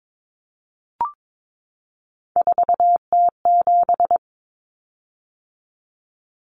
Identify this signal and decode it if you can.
Morse code, heard as a keyed tone, about 22 wpm, 700 Hz. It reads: 4T7
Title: Common Cut Numbers - 3-Digit 22wpm